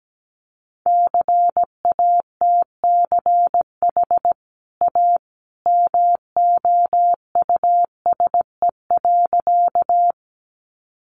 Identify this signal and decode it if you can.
Morse code, heard as a keyed tone, about 17 wpm, 700 Hz. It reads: CATCH A MOUSE.
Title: Sentences from Top 1500 Words 17wpm